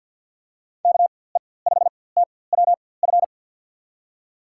Morse code, heard as a keyed tone, about 51 wpm, spaced as for 23 wpm, 700 Hz. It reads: KE5TWF
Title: USA Call Signs - 6 Characters - 4x Character Spacing - 50wpm